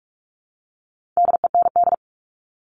Morse code, heard as a keyed tone, about 44 wpm, 700 Hz. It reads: DEND
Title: All Cut Numbers - 4-Digit 45wpm